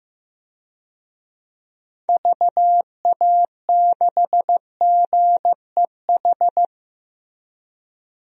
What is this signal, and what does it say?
Morse code, heard as a keyed tone, about 15 wpm, 700 Hz. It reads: VA6GEH
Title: DXCC-100 Call Signs - 6 Characters - Repeated 3x - 15wpm